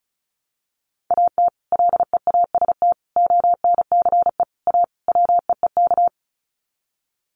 Morse code, heard as a keyed tone, about 34 wpm, 700 Hz. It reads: AT LEAST ONCE A WEEK
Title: Sets of 5 Words Encore2 35wpm